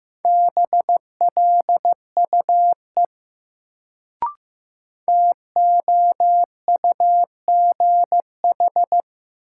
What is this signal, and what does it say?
Morse code, heard as a keyed tone, about 15 wpm, 700 Hz. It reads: BLUE TOUGH
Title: Sentences from Top 1200 Words - New Word Drill Alt - 15wpm